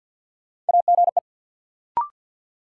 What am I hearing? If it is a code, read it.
Morse code, heard as a keyed tone, about 46 wpm, 700 Hz. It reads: AGE
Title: Sentences from Top 500 Words - New Word Drill - 50wpm